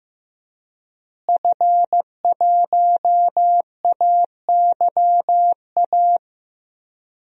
Morse code, heard as a keyed tone, about 15 wpm, 700 Hz. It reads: F1AYA
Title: DXCC-100 Call Signs - 5 Characters - Repeated 2x - 15wpm